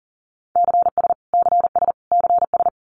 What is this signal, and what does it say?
Morse code, heard as a keyed tone, about 40 wpm, 700 Hz. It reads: CS CS CS
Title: Abbreviations Repeated 3x - 40wpm